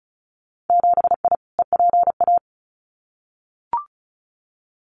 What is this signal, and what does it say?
Morse code, heard as a keyed tone, about 36 wpm, 700 Hz. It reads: 7I EPA
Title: Winter Field Day Exchanges Encore 35wpm